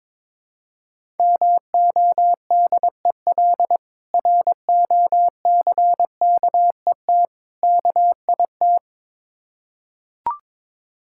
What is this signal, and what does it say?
Morse code, heard as a keyed tone, about 23 wpm, 690 Hz. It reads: MODEL ROCKET KIT
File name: Difficult Words with Hints - 22wpm